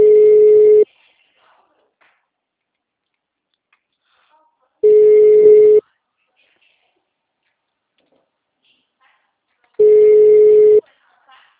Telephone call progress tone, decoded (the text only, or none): ringback tone